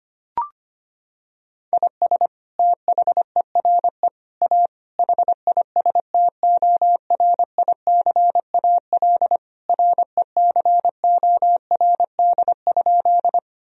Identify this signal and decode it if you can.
Morse code, heard as a keyed tone, about 25 wpm, 700 Hz. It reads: IS THERE A HISTORICAL RECORD?
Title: Sentences from Top 1000 Words - Review Part 4 - 25wpm